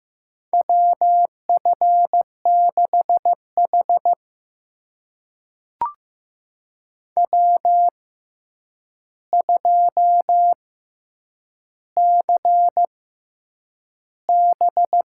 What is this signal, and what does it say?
Morse code, heard as a keyed tone, about 15 wpm, spaced as for 8 wpm, 700 Hz. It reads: WF6H W2CB
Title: USA Call Signs - 6x Character Spacing - 15wpm